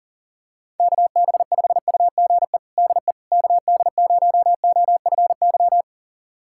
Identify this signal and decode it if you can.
Morse code, heard as a keyed tone, about 40 wpm, 700 Hz. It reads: KB5UGE DE KD0OFY
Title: USA Callsign DE Callsign 40wpm